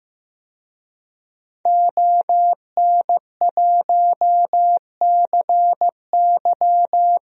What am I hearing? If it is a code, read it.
Morse code, heard as a keyed tone, about 15 wpm, 700 Hz. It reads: ON1CY
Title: DXCC-100 Call Signs - 5 Characters - 15wpm